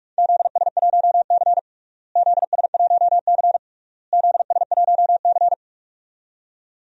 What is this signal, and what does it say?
Morse code, heard as a keyed tone, about 42 wpm, 700 Hz. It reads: ZS1C ZS1C ZS1C
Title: DXCC-100 Call Signs - 4 Characters - Repeated 3x - 45wpm